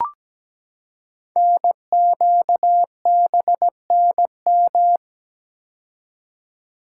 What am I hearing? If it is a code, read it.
Morse code, heard as a keyed tone, about 17 wpm, 700 Hz. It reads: NQBNM
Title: Code Groups - 5 Letters - 17wpm